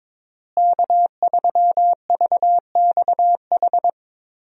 Morse code, heard as a keyed tone, about 22 wpm, 700 Hz. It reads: K3VXH
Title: USA Call Signs - 5 Characters - Repeated 2x - 22wpm